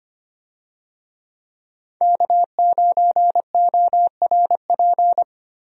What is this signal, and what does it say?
Morse code, heard as a keyed tone, about 25 wpm, 700 Hz. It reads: K9ORP
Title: USA Call Signs - 5 Characters - Repeated 2x - 25wpm